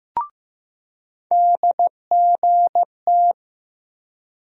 Morse code, heard as a keyed tone, about 15 wpm, 700 Hz. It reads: DGT
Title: USA Call Sign Postfixes - 3 Letters 15wpm